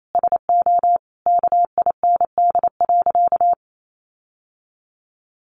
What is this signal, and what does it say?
Morse code, heard as a keyed tone, about 28 wpm, 700 Hz. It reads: SO KIND.
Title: Sentences from Top 1500 Words 28wpm